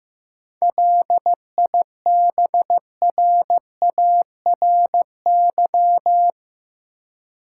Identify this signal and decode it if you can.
Morse code, heard as a keyed tone, about 15 wpm, 700 Hz. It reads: LIBRARY
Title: Sentences from Top 600 Words - New Word Drill - 15wpm